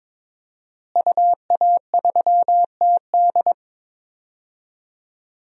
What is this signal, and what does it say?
Morse code, heard as a keyed tone, about 22 wpm, 700 Hz. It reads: UA3TD
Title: DXCC-100 Call Signs - 5 Characters - Repeated 3x - 22wpm